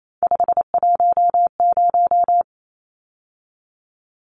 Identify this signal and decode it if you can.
Morse code, heard as a keyed tone, about 28 wpm, 700 Hz. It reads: 510
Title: Numbers - 3-Digit 28wpm